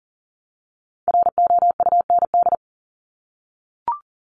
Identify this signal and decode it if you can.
Morse code, heard as a keyed tone, about 37 wpm, 690 Hz. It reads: ROUND
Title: Multi-letter Phonemes Repeated 3x 40wpm